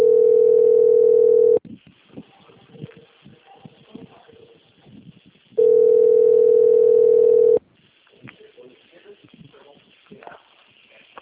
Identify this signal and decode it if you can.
Ringback tone (North America), 440 + 480 Hz, with speech.
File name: out- - - - - .19638